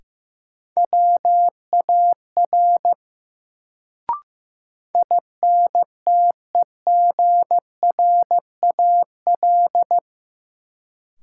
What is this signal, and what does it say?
Morse code, heard as a keyed tone, about 15 wpm, 700 Hz. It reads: WAR INTEGRAL
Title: Sets of 5 Words - New Word Drill - 15wpm